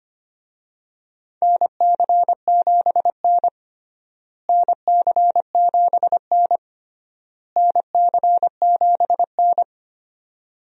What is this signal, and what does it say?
Morse code, heard as a keyed tone, about 25 wpm, 700 Hz. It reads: NC7N NC7N NC7N
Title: USA Call Signs - 4 Characters - Repeated 3x - 25wpm